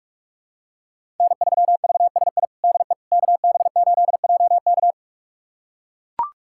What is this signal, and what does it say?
Morse code, heard as a keyed tone, about 42 wpm, 700 Hz. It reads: N3VSI DE KB8JK
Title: USA Callsign DE Callsign - 5 Characters - 45wpm